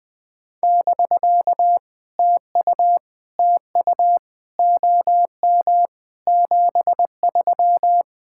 Morse code, heard as a keyed tone, about 20 wpm, 700 Hz. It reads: <BK> TU TU OM 73
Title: POTA QSO Practice - 20wpm